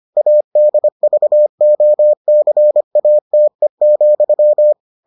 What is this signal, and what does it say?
Morse code, heard as a keyed tone, about 25 wpm, 600 Hz. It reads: ADVOCATE,